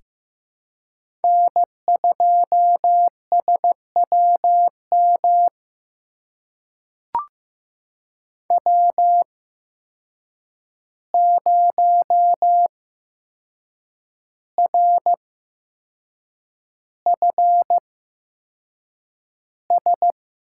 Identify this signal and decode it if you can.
Morse code, heard as a keyed tone, about 15 wpm, spaced as for 6 wpm, 700 Hz. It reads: N2SWM W0RFS
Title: USA Call Signs - 5 Characters - 8x Character Spacing - 15wpm